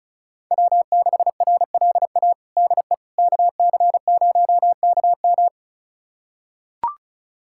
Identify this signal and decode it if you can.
Morse code, heard as a keyed tone, about 36 wpm, 700 Hz. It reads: W6RLA DE KC0KM
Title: USA Callsign DE Callsign - 5 Characters - 35wpm